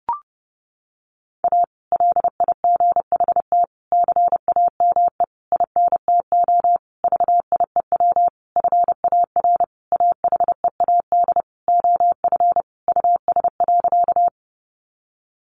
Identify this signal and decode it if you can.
Morse code, heard as a keyed tone, about 30 wpm, 700 Hz. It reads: A LIGHT CAME INTO VIEW FAR AHEAD OF US.
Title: Sentences from Top 1100 Words 30wpm